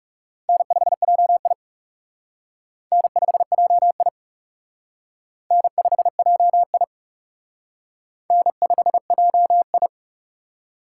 Morse code, heard as a keyed tone, about 36 wpm, 700 Hz. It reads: N5JI N5JI N5JI N5JI
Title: Speed Racing - USA Call Signs 30wpm